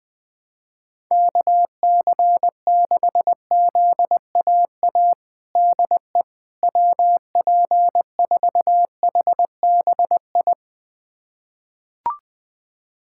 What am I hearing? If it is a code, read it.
Morse code, heard as a keyed tone, about 20 wpm, 700 Hz. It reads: KC6ZAA DE WP4HBI
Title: USA Callsign DE Callsign - 6 Characters - 20wpm